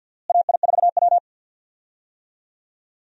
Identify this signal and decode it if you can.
Morse code, heard as a keyed tone, about 46 wpm, 700 Hz. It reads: AI4W
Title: USA Call Signs - 4 Characters - Repeated 2x - 50wpm